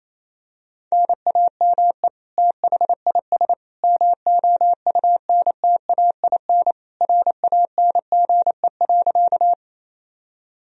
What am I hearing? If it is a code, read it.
Morse code, heard as a keyed tone, about 28 wpm, 700 Hz. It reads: NAME THIS MOUNTAIN RANGE.
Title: Sentences from Top 1000 Words - Review Part 4 - 28wpm